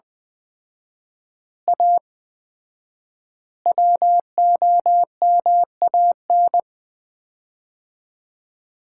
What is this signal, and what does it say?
Morse code, heard as a keyed tone, about 20 wpm, 700 Hz. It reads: A WOMAN
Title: 4x Word-Spacing - Sets of 5 Words - 20wpm